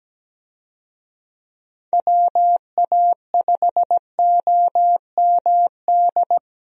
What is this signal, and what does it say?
Morse code, heard as a keyed tone, about 17 wpm, 700 Hz. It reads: WA5OMD
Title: USA Call Signs - 6 Characters - 17wpm